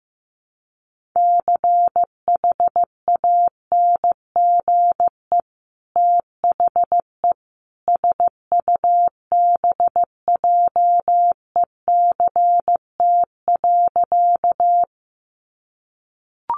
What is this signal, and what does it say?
Morse code, heard as a keyed tone, about 15 wpm, 700 Hz. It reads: CHANGE THE SUBJECT.
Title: Sentences from Top 800 Words 15wpm